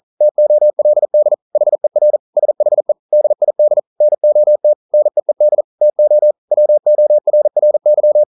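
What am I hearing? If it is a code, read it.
Morse code, heard as a keyed tone, about 38 wpm, 600 Hz. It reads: TOLD HER SHE DID NOT NEED TO WORRY
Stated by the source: Morse code practice at 41 wpm, from CW med Gnister.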